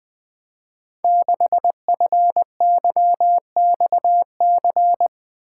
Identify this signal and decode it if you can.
Morse code, heard as a keyed tone, about 20 wpm, 700 Hz. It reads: 6FYXC
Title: Code Groups - 5 Letter:Numbers - 20wpm